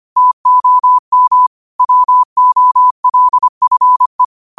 Morse code, heard as a keyed tone, about 25 wpm, 1000 Hz. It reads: TOM WOLFE
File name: qotdfiles / 25wpm